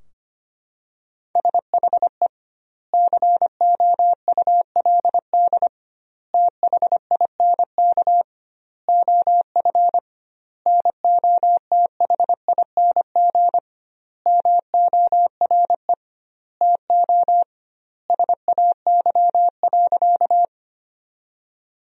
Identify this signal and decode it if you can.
Morse code, heard as a keyed tone, about 25 wpm, 700 Hz. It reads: SHE COULD THINK OF NOTHING MORE TO SAY.
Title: 2x Word-Spacing - Sentences from Top 500 Words - 25wpm